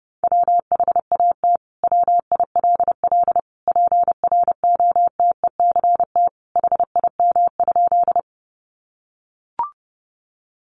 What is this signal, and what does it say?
Morse code, heard as a keyed tone, about 30 wpm, 700 Hz. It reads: WHAT WILL PROTECT HIM?
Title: Sentences from Top 900 Words 30wpm